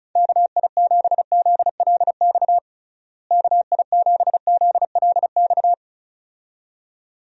Morse code, heard as a keyed tone, about 36 wpm, 700 Hz. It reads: KI7ZLX KI7ZLX
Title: USA Call Signs - 6 Characters - Repeated 2x - 35wpm